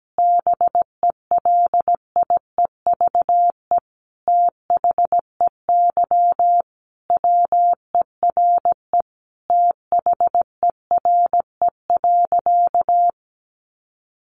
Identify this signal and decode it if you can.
Morse code, heard as a keyed tone, about 17 wpm, 700 Hz. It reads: BELIEVE THEY WERE THERE.
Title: Sentences from Top 1000 Words 17wpm